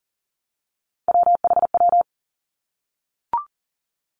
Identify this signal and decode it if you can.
Morse code, heard as a keyed tone, about 42 wpm, 700 Hz. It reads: WHW
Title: Three Challenging Letter-Number Combinations 40wpm